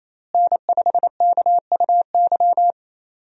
Morse code, heard as a keyed tone, about 28 wpm, 700 Hz. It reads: N5KUY
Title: USA Call Signs - 5 Characters - Repeated 3x - 28wpm